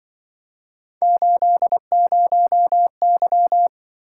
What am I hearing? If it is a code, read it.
Morse code, heard as a keyed tone, about 24 wpm, 700 Hz. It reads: 80Y
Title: Speed-Racing - Three Letter-Number Combinations - 20wpm